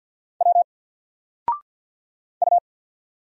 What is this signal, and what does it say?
Morse code, heard as a keyed tone, about 50 wpm, 700 Hz. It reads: W U
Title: Single Letter-Number Alt 50wpm